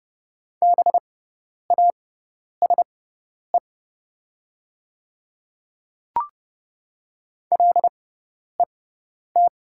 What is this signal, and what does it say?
Morse code, heard as a keyed tone, about 30 wpm, spaced as for 10 wpm, 700 Hz. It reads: BASE LET
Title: 6x Character Spacing - Top 500 Words - 30wpm